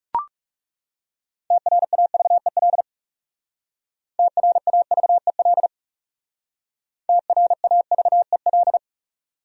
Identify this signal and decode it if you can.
Morse code, heard as a keyed tone, about 40 wpm, 700 Hz. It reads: TRAVEL TRAVEL TRAVEL